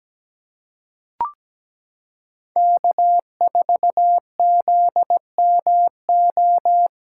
Morse code, heard as a keyed tone, about 17 wpm, 700 Hz. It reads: K4ZMO
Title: USA Callsign DE Callsign - 5 Characters - 17wpm